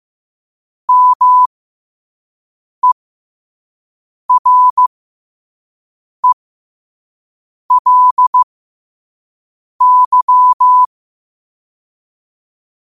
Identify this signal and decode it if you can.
Morse code, heard as a keyed tone, about 15 wpm, spaced as for 5 wpm, 1000 Hz. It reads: MERELY
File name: qotdfiles / 5wpm